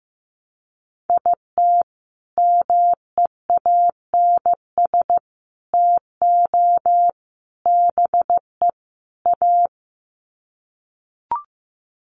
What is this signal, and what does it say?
Morse code, heard as a keyed tone, about 15 wpm, 700 Hz. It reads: IT MEANS TO BE A
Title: Sets of 5 Words Encore2 15wpm